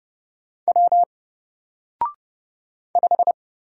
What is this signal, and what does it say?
Morse code, heard as a keyed tone, about 29 wpm, 700 Hz. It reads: W 5
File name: Single-Letter-Number-30wpm